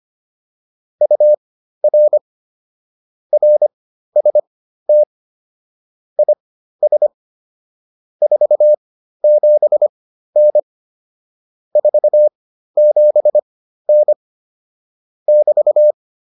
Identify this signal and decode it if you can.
Morse code, heard as a keyed tone, about 25 wpm, spaced as for 13 wpm, 600 Hz. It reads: UR RST IS 47N 47N <BT>